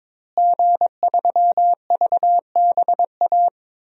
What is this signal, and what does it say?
Morse code, heard as a keyed tone, about 22 wpm, 700 Hz. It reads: G3VBA
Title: DXCC-100 Call Signs - 5 Characters - Repeated 3x - 22wpm